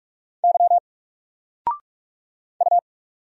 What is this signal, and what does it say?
Morse code, heard as a keyed tone, about 44 wpm, 700 Hz. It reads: Y U
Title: Single Letter-Number Alt 45wpm